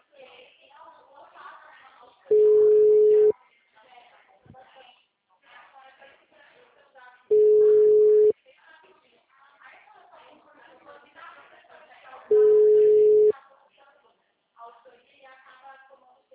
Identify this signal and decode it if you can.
Ringback tone (Europe, 425 Hz), with speech.